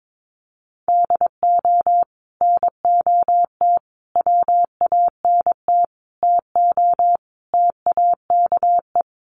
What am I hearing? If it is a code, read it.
Morse code, heard as a keyed tone, about 22 wpm, 700 Hz. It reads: DO NOT WANT TO TAKE
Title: Sets of 5 Words Encore2 22wpm